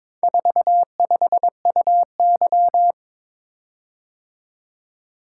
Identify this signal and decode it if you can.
Morse code, heard as a keyed tone, about 22 wpm, 700 Hz. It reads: 45UY
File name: practice / 22wpm / Four Letter-Number Combinations - 22wpm